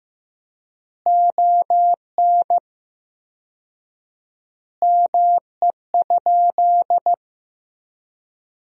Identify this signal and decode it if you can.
Morse code, heard as a keyed tone, about 15 wpm, 700 Hz. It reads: ON ME?
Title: 4x Word-Spacing - Sentences from Top 500 Words - 15wpm